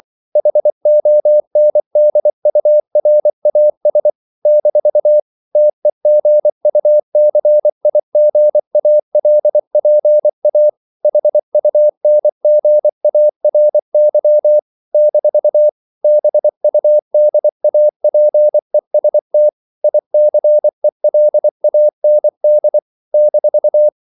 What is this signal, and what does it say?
Morse code, heard as a keyed tone, about 24 wpm, 600 Hz. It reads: HONDURAS - TEGUCIGALPA HUNGARY - BUDAPEST ICELAND -
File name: CW / Practice / NationalCapitals24